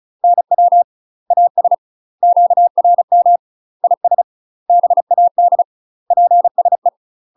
Morse code, heard as a keyed tone, about 35 wpm, 700 Hz. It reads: NW AS QRM IS BAD PSE
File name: UK MASTER QSOs TRAINING/35WPM